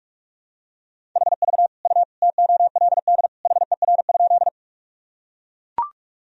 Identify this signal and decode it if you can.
Morse code, heard as a keyed tone, about 44 wpm, 700 Hz. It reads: HV U TOLD HER?
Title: Sentences from Top 400 Words and Abbreviations 45wpm